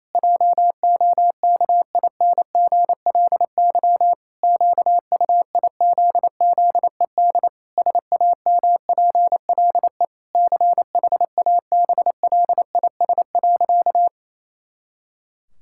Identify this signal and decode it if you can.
Morse code, heard as a keyed tone, about 28 wpm, 700 Hz. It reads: JOKINGLY QUIZZED SAMPLE CHABLIS.
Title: Pangram Challenge v2 - 28wpm